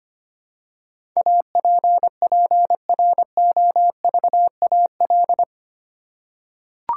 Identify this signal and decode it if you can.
Morse code, heard as a keyed tone, about 25 wpm, 690 Hz. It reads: APPROVAL
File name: 8 Letter Words - First 3 Letters Given - 25wpm